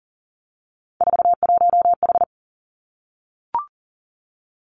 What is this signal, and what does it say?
Morse code, heard as a keyed tone, about 38 wpm, 700 Hz. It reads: 41H
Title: Three Challenging Letter-Number Combinations 40wpm